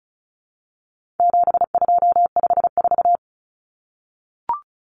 Morse code, heard as a keyed tone, about 34 wpm, 700 Hz. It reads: 7254
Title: Common Cut Numbers - 4-Digit 35wpm